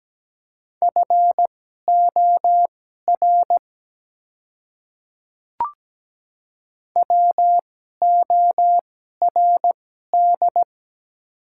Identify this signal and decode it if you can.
Morse code, heard as a keyed tone, about 17 wpm, spaced as for 12 wpm, 700 Hz. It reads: FOR WORD
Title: 2x Character Spacing - Top 100 Words - 17wpm